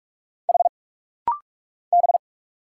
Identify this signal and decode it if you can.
Morse code, heard as a keyed tone, about 44 wpm, 700 Hz. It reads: H B
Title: Single Letters - Rapid-Fire 45wpm